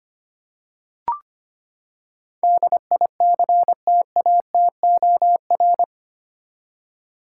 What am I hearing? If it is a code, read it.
Morse code, heard as a keyed tone, about 25 wpm, 700 Hz. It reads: DICTATOR
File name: Eight Letter Words - 25wpm